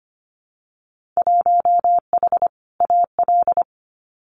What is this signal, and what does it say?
Morse code, heard as a keyed tone, about 25 wpm, 700 Hz. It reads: 1H AL
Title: Winter Field Day Exchanges 25wpm